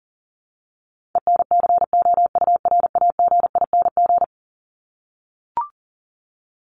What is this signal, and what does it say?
Morse code, heard as a keyed tone, about 40 wpm, 700 Hz. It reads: ENCOURAGING
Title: Top 1751-2000 Words 40wpm